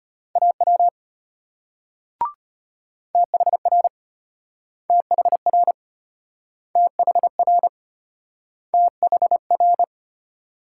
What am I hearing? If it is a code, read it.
Morse code, heard as a keyed tone, about 34 wpm, 700 Hz. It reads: AW THR THR THR THR